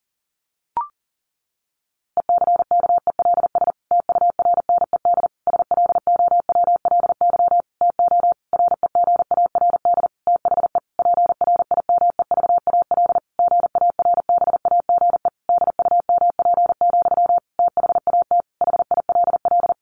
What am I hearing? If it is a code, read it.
Morse code, heard as a keyed tone, about 40 wpm, 700 Hz. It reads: ECKELS TURNED SLOWLY TO REGARD THE PRIMEVAL GARBAGE DUMP, THAT HILL
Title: A Sound Of Thunder 40wpm